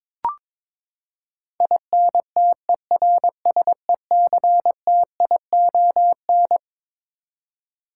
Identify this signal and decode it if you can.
Morse code, heard as a keyed tone, about 22 wpm, 700 Hz. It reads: INTERSECTION
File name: Twelve Letter Words - 22wpm